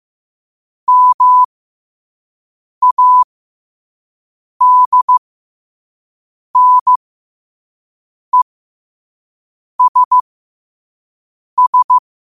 Morse code, heard as a keyed tone, about 15 wpm, spaced as for 5 wpm, 1000 Hz. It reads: MADNESS
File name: qotdfiles / 5wpm